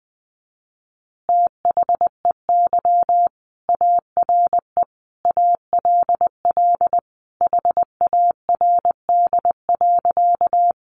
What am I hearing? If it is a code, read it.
Morse code, heard as a keyed tone, about 20 wpm, 700 Hz. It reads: THEY ARE ALL HARD.
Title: Additional Easy Phrases 20wpm